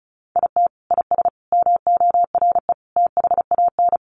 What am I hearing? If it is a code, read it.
Morse code, heard as a keyed tone, about 34 wpm, 700 Hz. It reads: IT IS MORE THAN
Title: Sentences from Top 1400 Words 35wpm